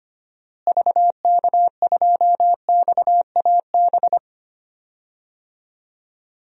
Morse code, heard as a keyed tone, about 25 wpm, 700 Hz. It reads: VK2XAB
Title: DXCC-100 Call Signs - 6 Characters - Repeated 2x - 25wpm